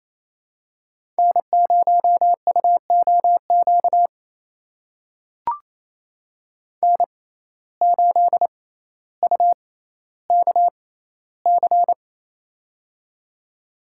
Morse code, heard as a keyed tone, about 28 wpm, spaced as for 14 wpm, 700 Hz. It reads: N0UOQ N8UKC
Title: USA Call Signs - 5 Characters - 6x Character Spacing - 28wpm